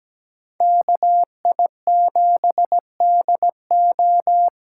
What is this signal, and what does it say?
Morse code, heard as a keyed tone, about 20 wpm, 700 Hz. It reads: KI7DO
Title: Speed Racing - USA 2x2 Call Signs 17wpm